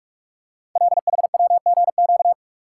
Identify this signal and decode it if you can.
Morse code, heard as a keyed tone, about 44 wpm, 700 Hz. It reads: LHWGQ
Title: Code Groups - 5 Letters - 45wpm